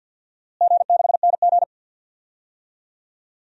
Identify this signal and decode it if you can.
Morse code, heard as a keyed tone, about 45 wpm, 700 Hz. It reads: G6NG
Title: DXCC-100 Call Signs - 4 Characters - Repeated 3x - 50wpm